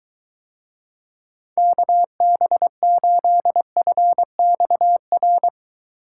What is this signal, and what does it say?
Morse code, heard as a keyed tone, about 23 wpm, 700 Hz. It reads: KB8FXR